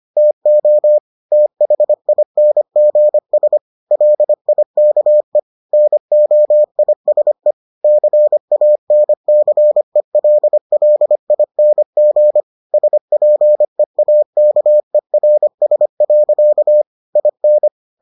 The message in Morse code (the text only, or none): TO THINGS LIKE NOISE CANCELLING SPEAKERS. IN